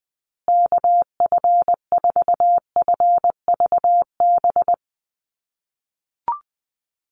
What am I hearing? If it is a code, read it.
Morse code, heard as a keyed tone, about 20 wpm, 700 Hz. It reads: KF4FVB
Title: USA Call Signs Repeated 3x 20wpm